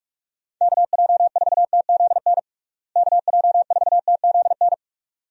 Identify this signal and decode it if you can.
Morse code, heard as a keyed tone, about 46 wpm, 700 Hz. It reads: KJ4TZN KJ4TZN
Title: USA Call Signs - 6 Characters - Repeated 2x - 45wpm